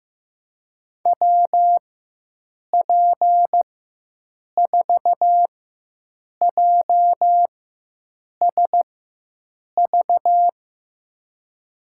Morse code, heard as a keyed tone, about 15 wpm, spaced as for 7 wpm, 700 Hz. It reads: WP4JSV
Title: USA Call Signs - 6 Characters - 4x Character Spacing - 15wpm